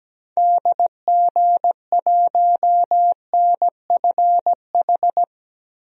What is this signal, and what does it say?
Morse code, heard as a keyed tone, about 17 wpm, 700 Hz. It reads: DG1NFH
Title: DXCC-100 Call Signs - 6 Characters - Repeated 3x - 17wpm